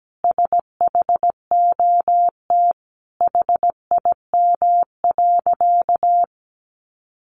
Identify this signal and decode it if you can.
Morse code, heard as a keyed tone, about 17 wpm, 700 Hz. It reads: SHOT HIM.
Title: Sentences from Top 1100 Words 17wpm